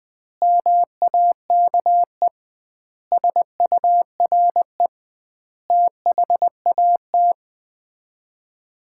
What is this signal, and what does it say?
Morse code, heard as a keyed tone, about 21 wpm, 700 Hz. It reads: MAKE SURE THAT
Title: 2x Word-Spacing - Sets of 5 Words - 20wpm